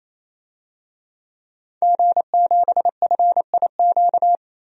Morse code, heard as a keyed tone, about 28 wpm, 700 Hz. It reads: G7FIQ
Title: DXCC-100 Call Signs - 5 Characters - 28wpm